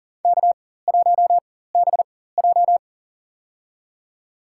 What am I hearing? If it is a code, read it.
Morse code, heard as a keyed tone, about 40 wpm, spaced as for 19 wpm, 700 Hz. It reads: K1BJ
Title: USA Call Signs - 4 Characters - 4x Character Spacing - 40wpm